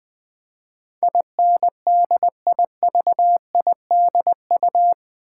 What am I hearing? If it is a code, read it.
Morse code, heard as a keyed tone, about 20 wpm, 700 Hz. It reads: INDIVIDU
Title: Build A Word - 4 Letters At A Time - 10-Letter Words - 20wpm